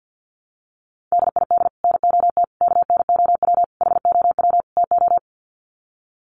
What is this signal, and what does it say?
Morse code, heard as a keyed tone, about 50 wpm, 700 Hz. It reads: DID NOT KNOW HOW TO
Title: Sets of 5 Words 50wpm